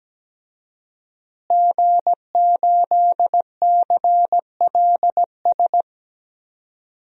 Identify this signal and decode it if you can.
Morse code, heard as a keyed tone, about 17 wpm, 700 Hz. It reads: G8CLS
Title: DXCC-100 Call Signs - 5 Characters - Repeated 3x - 17wpm